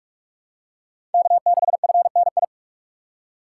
Morse code, heard as a keyed tone, about 43 wpm, 700 Hz. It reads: K6FNI
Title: USA Call Signs - 5 Characters - 3x Character Spacing - 45wpm